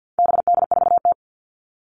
Morse code, heard as a keyed tone, about 50 wpm, 700 Hz. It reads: BD4T
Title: All Cut Numbers - 4-Digit 50wpm